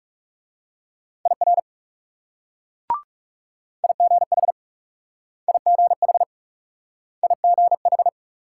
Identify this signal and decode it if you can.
Morse code, heard as a keyed tone, about 38 wpm, 700 Hz. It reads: IR IGH IGH IGH